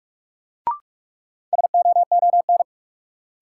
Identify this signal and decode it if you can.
Morse code, heard as a keyed tone, about 45 wpm, 700 Hz. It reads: SOON
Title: Sentences from Top 400 Words - New Word Drill - 45wpm